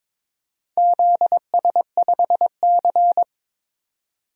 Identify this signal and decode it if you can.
Morse code, heard as a keyed tone, about 22 wpm, 700 Hz. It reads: ZS5C
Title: DXCC-100 Call Signs - 4 Characters - 22wpm